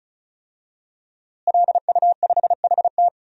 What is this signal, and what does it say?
Morse code, heard as a keyed tone, about 36 wpm, 700 Hz. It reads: LU5HT
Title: DXCC-100 Call Signs - 5 Characters - Repeated 2x - 35wpm